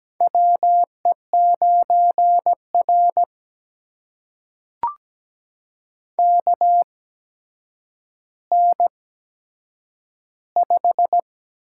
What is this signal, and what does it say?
Morse code, heard as a keyed tone, about 17 wpm, spaced as for 8 wpm, 700 Hz. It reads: WE9R KN5
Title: USA Call Signs - 4 Characters - 8x Character Spacing - 17wpm